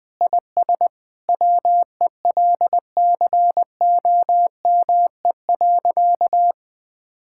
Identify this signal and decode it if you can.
Morse code, heard as a keyed tone, about 21 wpm, 700 Hz. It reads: IS WELCOME.
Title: Sentences with 7-Word Phrases - 20wpm